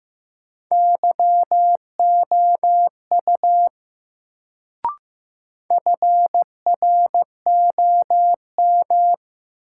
Morse code, heard as a keyed tone, about 15 wpm, 700 Hz. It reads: YOU FROM
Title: Sets of 2 Words - New Word Drill - 15wpm